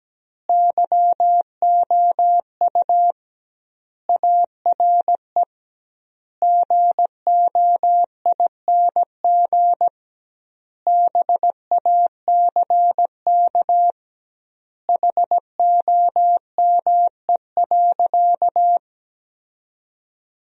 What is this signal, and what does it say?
Morse code, heard as a keyed tone, about 17 wpm, 700 Hz. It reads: YOU ARE GOING BACK HOME.
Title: 2x Word-Spacing - Sentences from Top 500 Words - 17wpm